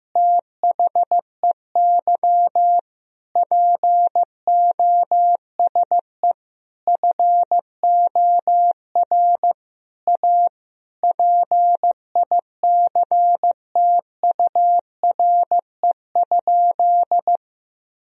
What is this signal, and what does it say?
Morse code, heard as a keyed tone, about 15 wpm, 700 Hz. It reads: THEY POSE FOR A PICTURE?
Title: Sentences from Top 1000 Words - Review Part 2 - 15wpm